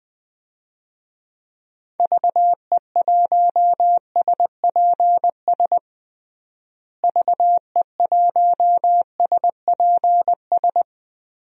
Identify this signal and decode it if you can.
Morse code, heard as a keyed tone, about 20 wpm, 700 Hz. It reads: VE1SPS VE1SPS
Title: DXCC-100 Call Signs - 6 Characters - Repeated 2x - 20wpm